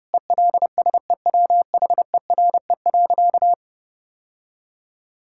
Morse code, heard as a keyed tone, about 30 wpm, 700 Hz. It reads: ELSEWHERE.
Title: Sentences from Top 1600 Words 30wpm